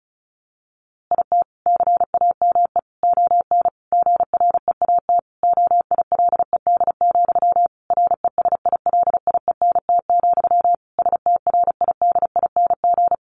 Morse code, heard as a keyed tone, about 34 wpm, 700 Hz. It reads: IT CAME ON GREAT OILED, RESILIENT, STRIDING
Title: A Sound Of Thunder 35wpm